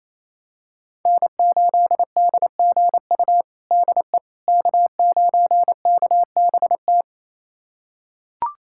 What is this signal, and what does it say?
Morse code, heard as a keyed tone, about 28 wpm, 700 Hz. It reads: N8DGU DE K9KBT
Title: USA Callsign DE Callsign - 5 Characters - 28wpm